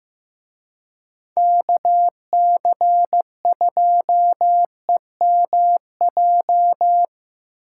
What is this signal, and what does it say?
Morse code, heard as a keyed tone, about 17 wpm, 700 Hz. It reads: KC2EMJ
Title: Speed Racing - USA 2x3 Call Signs 15wpm